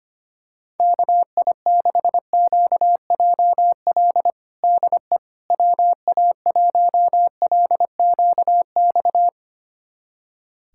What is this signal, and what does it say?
Morse code, heard as a keyed tone, about 25 wpm, 700 Hz. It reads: KI6QJL DE WA1LQX
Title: USA Callsign DE Callsign - 6 Characters - 25wpm